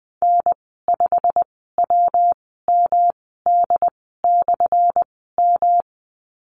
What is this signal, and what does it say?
Morse code, heard as a keyed tone, about 20 wpm, spaced as for 14 wpm, 700 Hz. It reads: N5WMD/M
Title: 179 Lesson 29g - Slash - Call Signs 20wpm